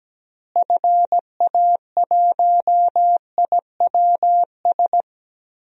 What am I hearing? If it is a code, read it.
Morse code, heard as a keyed tone, about 17 wpm, 700 Hz. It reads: FA1IWS
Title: DXCC-100 Call Signs - 6 Characters - Repeated 3x - 17wpm